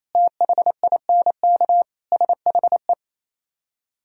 Morse code, heard as a keyed tone, about 36 wpm, 700 Hz. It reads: THINK SHE